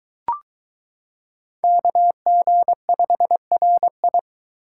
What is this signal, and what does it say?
Morse code, heard as a keyed tone, about 22 wpm, 700 Hz. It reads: KG5RI